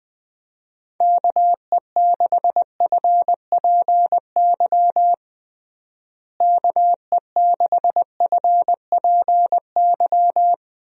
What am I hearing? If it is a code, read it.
Morse code, heard as a keyed tone, about 20 wpm, 700 Hz. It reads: KE6FPY KE6FPY
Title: USA Call Signs - 6 Characters - Repeated 3x - 20wpm